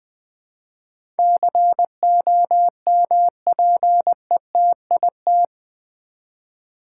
Morse code, heard as a keyed tone, about 20 wpm, 700 Hz. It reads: COMPETIT
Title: Build A Word - 4 Letters At A Time - 11-Letter Words - 20wpm